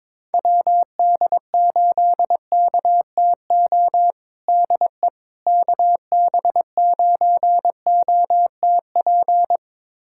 Morse code, heard as a keyed tone, about 22 wpm, 700 Hz. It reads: WD8KTO DE KB9OTP
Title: USA Callsign DE Callsign - 6 Characters - 22wpm